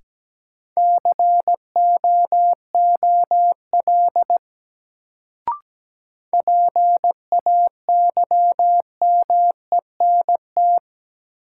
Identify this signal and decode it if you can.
Morse code, heard as a keyed tone, about 17 wpm, 700 Hz. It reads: COOL PAYMENT
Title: Sentences from Top 1200 Words - New Word Drill Alt - 17wpm